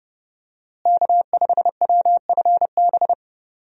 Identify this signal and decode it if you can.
Morse code, heard as a keyed tone, about 30 wpm, 700 Hz. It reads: K5WFB
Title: USA Call Signs - 5 Characters - Repeated 2x - 30wpm